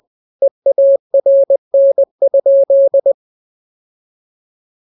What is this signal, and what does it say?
Morse code, heard as a keyed tone, about 20 wpm, 550 Hz. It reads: EARN?
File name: jokes / joke 2 20 0000